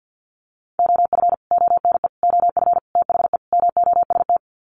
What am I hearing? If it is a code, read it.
Morse code, heard as a keyed tone, about 50 wpm, 700 Hz. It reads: OF ONE OF THE MOST
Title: Sets of 5 Words Encore2 50wpm